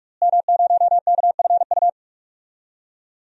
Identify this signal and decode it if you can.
Morse code, heard as a keyed tone, about 42 wpm, 700 Hz. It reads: M0KFU
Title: DXCC-100 Call Signs - 5 Characters - Repeated 2x - 45wpm